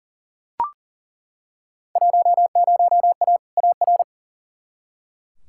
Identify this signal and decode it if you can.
Morse code, heard as a keyed tone, about 40 wpm, 700 Hz. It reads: 10A AR
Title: Field Day Exchanges Encore - 40wpm